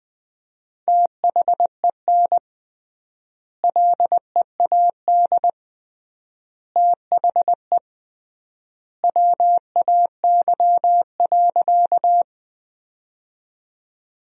Morse code, heard as a keyed tone, about 21 wpm, 700 Hz. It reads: THEN LEAD THE WAY.
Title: 3x Word-Spacing - Sentences from Top 500 Words - 20wpm